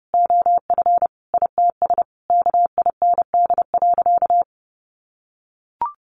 Morse code, heard as a keyed tone, about 30 wpm, 700 Hz. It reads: OF ITS KIND.
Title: Sentences from Top 1000 Words 30wpm